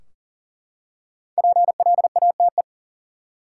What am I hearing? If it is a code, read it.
Morse code, heard as a keyed tone, about 40 wpm, 700 Hz. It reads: PLATE
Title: Five Letter Words - 40wpm